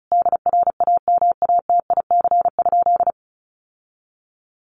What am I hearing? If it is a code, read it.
Morse code, heard as a keyed tone, about 34 wpm, 700 Hz. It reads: DRAMATIC?
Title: Sentences from Top 1300 Words 35wpm